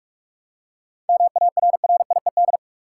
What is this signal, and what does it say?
Morse code, heard as a keyed tone, about 45 wpm, 700 Hz. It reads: MARRIED
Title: Seven Letter Words - 45wpm